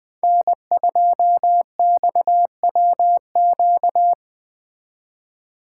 Morse code, heard as a keyed tone, about 20 wpm, 700 Hz. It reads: N2XWQ